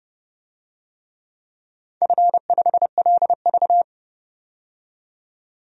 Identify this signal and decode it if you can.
Morse code, heard as a keyed tone, about 29 wpm, 700 Hz. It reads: F5LV
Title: DXCC-100 Call Signs - 4 Characters - 30wpm